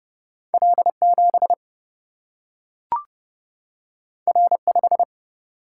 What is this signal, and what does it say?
Morse code, heard as a keyed tone, about 32 wpm, 700 Hz. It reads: L 7 R 5
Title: Single-Letter Single-Number 30wpm